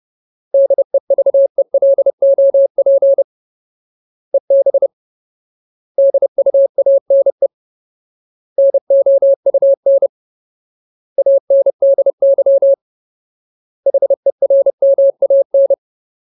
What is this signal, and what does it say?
Morse code, heard as a keyed tone, about 30 wpm, 550 Hz. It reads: DEVELOP EB DUANE NOUN ANDY HERMAN
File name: Ultimate Word List / 30WPM Ultimate